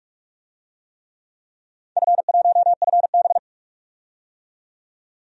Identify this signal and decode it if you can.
Morse code, heard as a keyed tone, about 46 wpm, 700 Hz. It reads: F1FB
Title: DXCC-100 Call Signs - 4 Characters - 45wpm